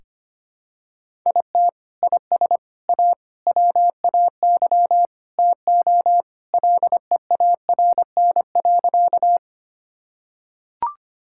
Morse code, heard as a keyed tone, about 25 wpm, 700 Hz. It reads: IT IS A WAY TO LEARN.
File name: practice / 25wpm / Sentences with 3-Word Phrases - 25wpm